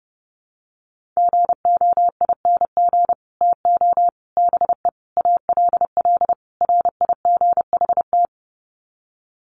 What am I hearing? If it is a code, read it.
Morse code, heard as a keyed tone, about 30 wpm, 700 Hz. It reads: GOING TO BE ALL RIGHT
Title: Sets of 5 Words Encore2 30wpm